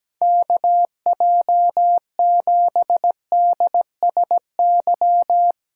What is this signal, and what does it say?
Morse code, heard as a keyed tone, about 17 wpm, 700 Hz. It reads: KJ7DSY
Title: K1USN SST QSO Practice 17wpm